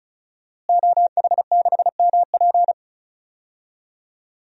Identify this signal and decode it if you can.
Morse code, heard as a keyed tone, about 36 wpm, 700 Hz. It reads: OH6MP
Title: DXCC-100 Call Signs - 5 Characters - Repeated 3x - 35wpm